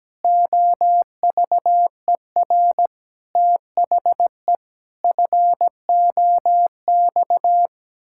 Morse code, heard as a keyed tone, about 17 wpm, 700 Hz. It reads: OVER THE FOX
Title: Pangram Challenge v2 - 17wpm